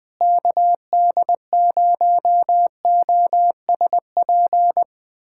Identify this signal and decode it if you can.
Morse code, heard as a keyed tone, about 20 wpm, 700 Hz. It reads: KD0OSP
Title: USA Callsign DE Callsign 20wpm